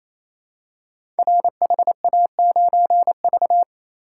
Morse code, heard as a keyed tone, about 28 wpm, 700 Hz. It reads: RHA9V
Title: Code Groups - 5 Letter:Numbers - 28wpm